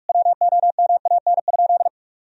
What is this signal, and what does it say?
Morse code, heard as a keyed tone, about 46 wpm, 700 Hz. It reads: WOMAN?
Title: Sentences from Top 1000 Words - Review Part 3 - 45wpm